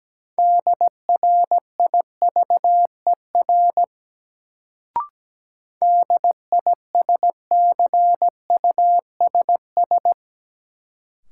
Morse code, heard as a keyed tone, about 17 wpm, 700 Hz. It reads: DRIVER DISCUSS
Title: Sentences from Top 1000 Words - New Word Drill - 17wpm